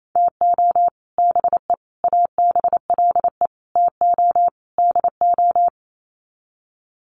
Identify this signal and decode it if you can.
Morse code, heard as a keyed tone, about 28 wpm, 700 Hz. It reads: TO BE ABLE TO DO
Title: Sets of 5 Words 28wpm